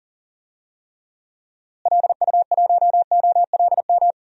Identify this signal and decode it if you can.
Morse code, heard as a keyed tone, about 38 wpm, 700 Hz. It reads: LU1OLM
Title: DXCC-100 Call Signs - 6 Characters - Repeated 3x - 40wpm